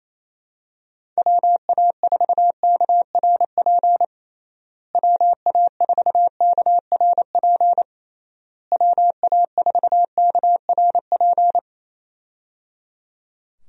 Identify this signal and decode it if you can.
Morse code, heard as a keyed tone, about 28 wpm, 700 Hz. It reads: WA4KRP WA4KRP WA4KRP
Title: USA Call Signs - 6 Characters - Repeated 3x - 28wpm